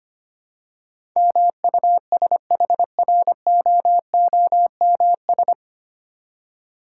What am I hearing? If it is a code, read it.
Morse code, heard as a keyed tone, about 25 wpm, 700 Hz. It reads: MUSHROOMS
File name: practice / 25wpm / Difficult Words with Hints - 25wpm